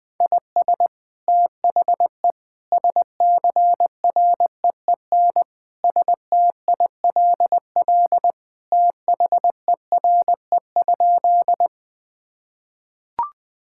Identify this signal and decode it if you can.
Morse code, heard as a keyed tone, about 21 wpm, 700 Hz. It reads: IS THE SCREEN STILL THERE?
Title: Sentences from Top 1000 Words - Review Part 6 - 20wpm